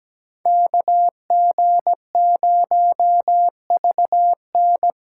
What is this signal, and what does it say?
Morse code, heard as a keyed tone, about 20 wpm, 700 Hz. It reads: KG0VN
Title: Speed Racing - USA 2x2 Call Signs 17wpm